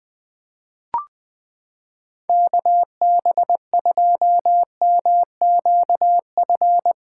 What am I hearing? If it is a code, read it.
Morse code, heard as a keyed tone, about 20 wpm, 700 Hz. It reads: KB2MQF
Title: USA Callsign DE Callsign 20wpm